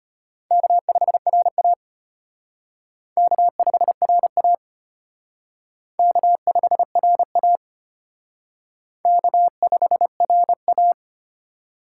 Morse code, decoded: K5RA K5RA K5RA K5RA

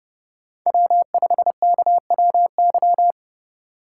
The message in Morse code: W5KWY

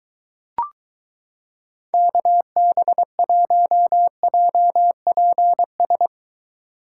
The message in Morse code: KB1JPS

